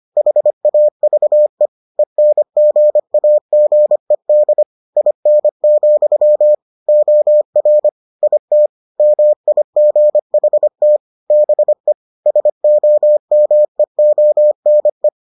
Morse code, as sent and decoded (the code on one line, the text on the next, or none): .... .- ...- . / . -. --. .- --. -.. / .. -. --..-- / --- .-. / .. - / -- .. --. .... - / -... / ... --- -- . --- -. .
HAVE ENGAGED IN, OR IT MIGHT BE SOMEONE